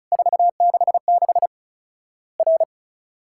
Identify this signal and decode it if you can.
Morse code, heard as a keyed tone, about 34 wpm, 700 Hz. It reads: 466 R
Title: Worked All Europe DX Contest - QTC Exchange - 35wpm